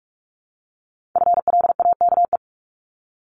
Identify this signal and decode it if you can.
Morse code, heard as a keyed tone, about 46 wpm, 700 Hz. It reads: FLAKE
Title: Multi-letter Phonemes 45wpm